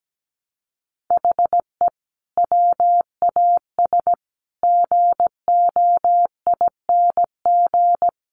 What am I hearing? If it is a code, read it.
Morse code, heard as a keyed tone, about 17 wpm, 700 Hz. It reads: HE WAS GOING
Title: Sets of 5 Words Encore2 17wpm